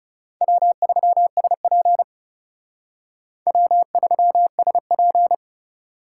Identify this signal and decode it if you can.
Morse code, heard as a keyed tone, about 39 wpm, 700 Hz. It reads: W3SP W3SP